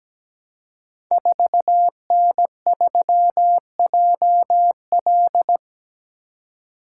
Morse code, heard as a keyed tone, about 17 wpm, 700 Hz. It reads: 4N3JL
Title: Code Groups - 5 Letter:Numbers - 17wpm